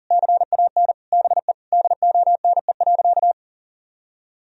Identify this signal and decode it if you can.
Morse code, heard as a keyed tone, about 40 wpm, 700 Hz. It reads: CAN BE DONE.
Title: Sentences from Top 1000 Words - Review Part 1 - 40wpm